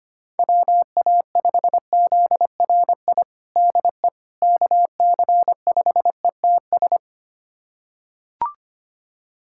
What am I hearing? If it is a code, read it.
Morse code, heard as a keyed tone, about 25 wpm, 700 Hz. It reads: WA5ZRI DE KC5ETS
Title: USA Callsign DE Callsign 25wpm